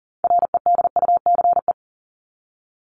Morse code, heard as a keyed tone, about 40 wpm, 700 Hz. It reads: REDUCE
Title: ED Suffix - Top 100 Words with Base Words 40wpm